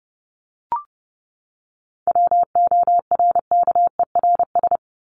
Morse code, heard as a keyed tone, about 30 wpm, 700 Hz. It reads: WORKERS
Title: ES:S Suffix - Top 100 Words with Base Words 30wpm